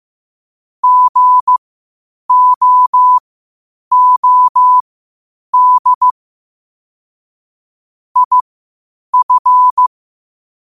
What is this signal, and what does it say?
Morse code, heard as a keyed tone, about 15 wpm, spaced as for 8 wpm, 1000 Hz. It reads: GOOD IF
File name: qotdfiles / 8wpm